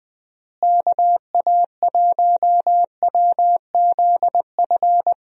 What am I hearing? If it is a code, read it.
Morse code, heard as a keyed tone, about 20 wpm, 700 Hz. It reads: KA1WZF
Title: USA Call Signs - 6 Characters - Repeated 3x - 20wpm